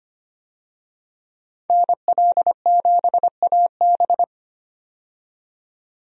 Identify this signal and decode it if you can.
Morse code, heard as a keyed tone, about 25 wpm, 700 Hz. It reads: NL7AB
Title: USA Call Signs - 5 Characters - 25wpm